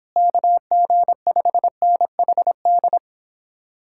Morse code, heard as a keyed tone, about 26 wpm, 700 Hz. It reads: KG5NHD
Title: Speed-Racing Common QSO Phrases Encore - 22wpm